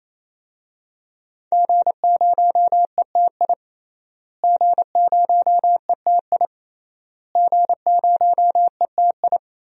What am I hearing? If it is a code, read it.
Morse code, heard as a keyed tone, about 28 wpm, 700 Hz. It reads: G0ETI G0ETI G0ETI
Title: DXCC-100 Call Signs - 5 Characters - Repeated 3x - 28wpm